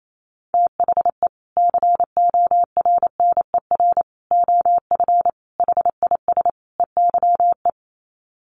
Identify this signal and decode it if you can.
Morse code, heard as a keyed tone, about 28 wpm, 700 Hz. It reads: THE CORNER OF HIS EYE
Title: Sets of 5 Words Encore2 28wpm